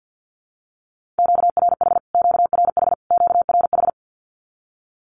Morse code, heard as a keyed tone, about 50 wpm, 700 Hz. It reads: QRH QRH QRH
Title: Q-Codes Repeated 3x 50wpm